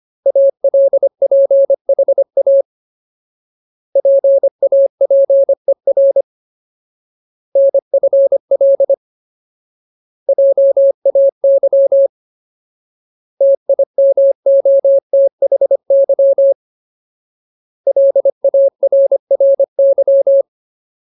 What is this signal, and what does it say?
Morse code, heard as a keyed tone, about 25 wpm, 550 Hz. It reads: ALPHA PAPER NFL JAY TIMOTHY LARRY
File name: Ultimate Word List / 25WPM Ultimate